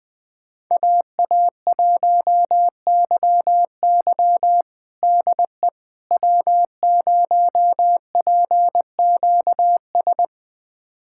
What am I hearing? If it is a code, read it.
Morse code, heard as a keyed tone, about 20 wpm, 700 Hz. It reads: AA1YY DE W0PQS
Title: USA Callsign DE Callsign - 5 Characters - 20wpm